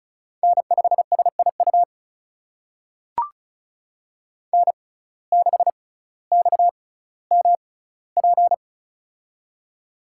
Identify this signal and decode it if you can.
Morse code, heard as a keyed tone, about 34 wpm, spaced as for 18 wpm, 700 Hz. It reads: N5SIU N6XMP
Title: USA Call Signs - 5 Characters - 6x Character Spacing - 35wpm